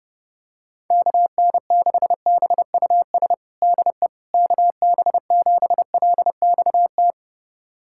Morse code, heard as a keyed tone, about 32 wpm, 700 Hz. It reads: KN6BUS DE KB7LXT
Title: USA Callsign DE Callsign - 6 Characters - 30wpm